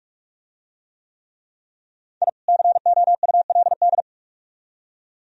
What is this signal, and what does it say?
Morse code, heard as a keyed tone, about 44 wpm, 700 Hz. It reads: I COULD